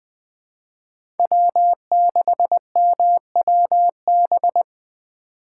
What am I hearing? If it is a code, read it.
Morse code, heard as a keyed tone, about 20 wpm, 700 Hz. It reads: W6MWB
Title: USA Call Signs - 5 Characters - 4x Character Spacing - 20wpm